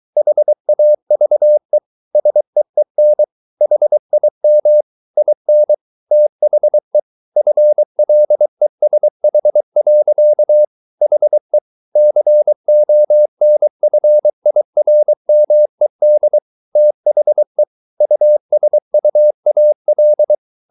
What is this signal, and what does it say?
Morse code, heard as a keyed tone, about 23 wpm, 600 Hz. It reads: HAVE SEEN HIM IN THE FLESH. HE CONFIRMED THE USUAL